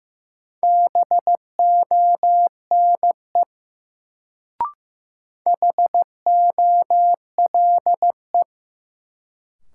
Morse code, heard as a keyed tone, about 15 wpm, 700 Hz. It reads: BONE HOLE
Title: Sentences from Top 1600 Words - New Word Drill Alt - 15wpm